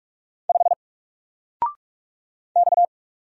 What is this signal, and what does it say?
Morse code, heard as a keyed tone, about 46 wpm, 700 Hz. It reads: 5 X
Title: Single Letter-Number 45wpm